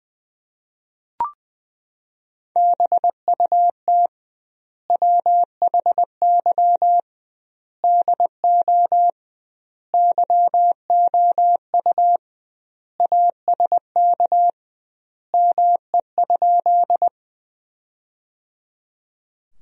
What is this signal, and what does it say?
Morse code, heard as a keyed tone, about 20 wpm, 700 Hz. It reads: BUT WHY DO YOU ASK ME?
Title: 2x Word-Spacing - Sentences from Top 200 Words - 20wpm